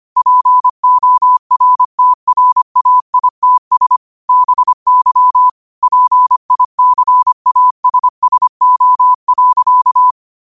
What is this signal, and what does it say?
Morse code, heard as a keyed tone, about 25 wpm, 1000 Hz. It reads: PORTRAITS BY PICASSO.